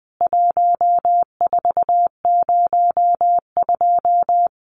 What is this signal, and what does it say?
Morse code, heard as a keyed tone, about 20 wpm, 700 Hz. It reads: 1402
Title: CWT Exchange 20wpm